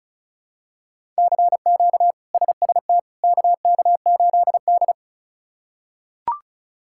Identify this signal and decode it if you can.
Morse code, heard as a keyed tone, about 34 wpm, 700 Hz. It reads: CQ SST KK8D
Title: K1USN SST QSO Practice - 35wpm